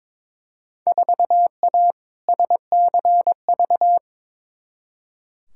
Morse code, heard as a keyed tone, about 21 wpm, 700 Hz. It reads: 4A SCV
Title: Field Day Exchanges Encore - 22wpm